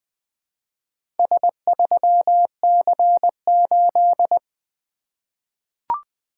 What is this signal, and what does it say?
Morse code, heard as a keyed tone, about 20 wpm, 700 Hz. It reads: S3C8
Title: Four Letter-Number Combinations - 20wpm